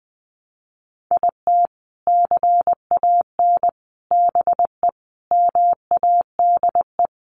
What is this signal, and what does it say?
Morse code, heard as a keyed tone, about 20 wpm, 700 Hz. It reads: IT CAN BE MADE
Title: Sentences from Top 1200 Words 20wpm